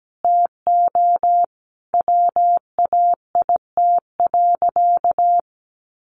Morse code, heard as a keyed tone, about 17 wpm, 700 Hz. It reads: TO WAIT.
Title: Sentences from Top 800 Words 17wpm